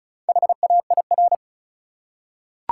Morse code, heard as a keyed tone, about 35 wpm, 700 Hz. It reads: HAIR
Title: Four Letter Words - 35wpm